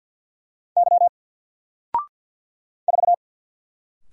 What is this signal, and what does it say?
Morse code, heard as a keyed tone, about 50 wpm, 700 Hz. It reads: Y 4
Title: Single Letter-Number 50wpm